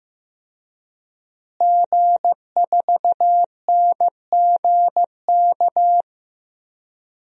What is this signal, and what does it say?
Morse code, heard as a keyed tone, about 15 wpm, 700 Hz. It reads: G4NGK
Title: DXCC-100 Call Signs - 5 Characters - Repeated 3x - 15wpm